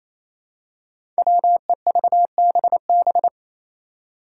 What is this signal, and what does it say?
Morse code, heard as a keyed tone, about 28 wpm, 700 Hz. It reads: WEVBB
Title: Five Challenging Letter-Number Combinations 28wpm